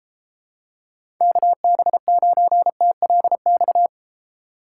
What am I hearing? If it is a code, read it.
Morse code, heard as a keyed tone, about 32 wpm, 700 Hz. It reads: KB9TLX